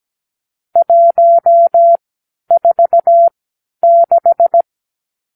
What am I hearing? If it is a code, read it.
Morse code, heard as a keyed tone, about 17 wpm, spaced as for 10 wpm, 680 Hz. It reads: 146
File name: k7qo / Lesson 50 (1-6)